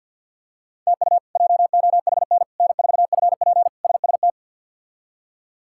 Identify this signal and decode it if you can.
Morse code, heard as a keyed tone, about 51 wpm, 700 Hz. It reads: TU JOHN N4FP SST
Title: K1USN SST QSO Practice - 50wpm